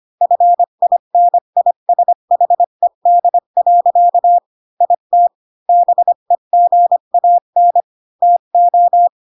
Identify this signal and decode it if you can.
Morse code, heard as a keyed tone, about 25 wpm, 700 Hz. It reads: FINISHED. IT BEGAN TO
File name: Pinocchio / Pinocchio ch3 20 25 ik1hge